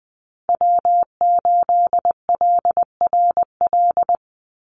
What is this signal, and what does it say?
Morse code, heard as a keyed tone, about 19 wpm, 700 Hz. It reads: W8LRL
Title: USA Call Signs 20wpm